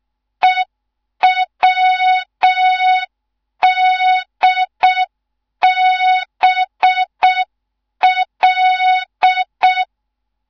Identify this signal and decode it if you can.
Morse code, heard as a keyed tone, about 6 wpm, 3700 Hz. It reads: EWDBL
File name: Morze-HA2MN / hang / 15 GYAKORLAT